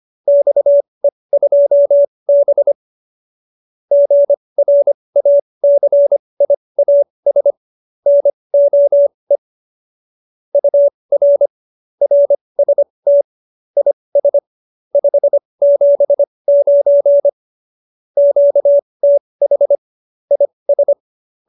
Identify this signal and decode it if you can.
Morse code, heard as a keyed tone, about 25 wpm, spaced as for 20 wpm, 570 Hz. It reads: XE2B GRACIAS NOE UR RST IS 579 QTH IS